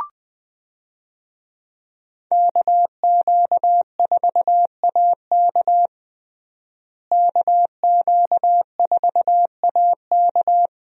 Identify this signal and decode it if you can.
Morse code, heard as a keyed tone, about 20 wpm, 700 Hz. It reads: KQ4AK KQ4AK
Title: USA Call Signs - 5 Characters - Repeated 2x - 20wpm